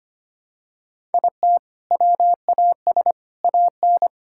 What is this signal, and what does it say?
Morse code, heard as a keyed tone, about 25 wpm, 700 Hz. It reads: IT WAS AN
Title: Sentences from Top 1000 Words - Review Part 6 - 25wpm